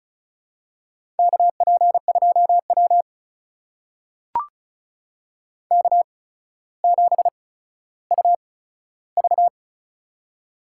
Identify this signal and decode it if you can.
Morse code, heard as a keyed tone, about 36 wpm, spaced as for 15 wpm, 700 Hz. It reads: KP2W K7UV
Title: USA Call Signs - 4 Characters - 8x Character Spacing - 35wpm